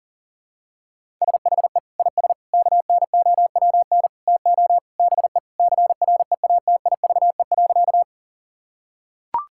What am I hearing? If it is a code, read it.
Morse code, heard as a keyed tone, about 40 wpm, 700 Hz. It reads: SHE IS KNOWN TO BE CREATIVE.
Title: Sentences from Top 1000 Words - Review Part 5 - 40wpm